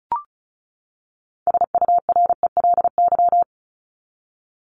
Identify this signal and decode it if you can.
Morse code, heard as a keyed tone, about 34 wpm, 700 Hz. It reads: SURELY
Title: LY Suffix - Top 100 Words with Base Words 35wpm